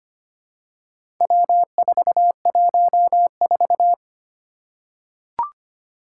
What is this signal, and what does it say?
Morse code, heard as a keyed tone, about 25 wpm, 700 Hz. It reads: W414
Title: Four Challenging Letter-Number Combinations 25wpm